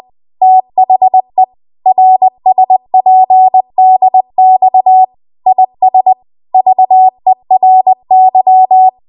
Morse code, heard as a keyed tone, about 20 wpm, 750 Hz. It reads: THE RSPDX IS VERY
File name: Archive / 20 WPM / 20WPM